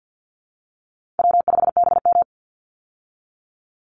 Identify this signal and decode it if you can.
Morse code, heard as a keyed tone, about 50 wpm, 700 Hz. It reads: W5BM
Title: Easy Call Signs Repeated 3x - 50wpm